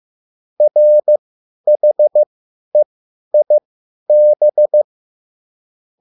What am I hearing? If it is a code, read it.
Morse code, heard as a keyed tone, about 15 wpm, spaced as for 10 wpm, 600 Hz. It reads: RHEIB